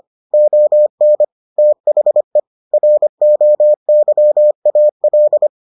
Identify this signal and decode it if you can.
Morse code, heard as a keyed tone, about 25 wpm, 600 Hz. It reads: ON THE ROYAL